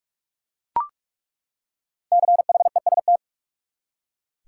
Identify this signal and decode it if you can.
Morse code, heard as a keyed tone, about 41 wpm, 700 Hz. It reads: CHEST
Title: Five Letter Words - 45wpm